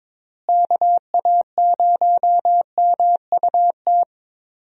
Morse code, heard as a keyed tone, about 21 wpm, 700 Hz. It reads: KA0MUT
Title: USA Call Signs - 6 Characters - Repeated 3x - 22wpm